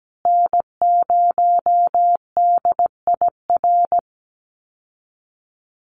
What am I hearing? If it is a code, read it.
Morse code, heard as a keyed tone, about 17 wpm, 700 Hz. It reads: N0DIR
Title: Call Signs Repeated 2x - 17wpm